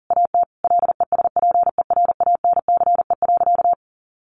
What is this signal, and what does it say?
Morse code, heard as a keyed tone, about 40 wpm, 700 Hz. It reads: AT LESPERANCE.
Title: A Sound Of Thunder 40wpm